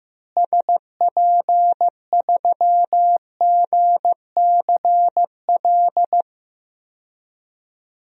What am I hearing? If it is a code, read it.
Morse code, heard as a keyed tone, about 15 wpm, 700 Hz. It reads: SP3GCL
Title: DXCC-100 Call Signs - 6 Characters - Repeated 2x - 15wpm